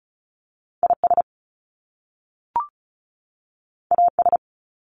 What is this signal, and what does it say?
Morse code, heard as a keyed tone, about 35 wpm, 700 Hz. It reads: IS AS
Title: Top Two Letter Combinations in Words 35wpm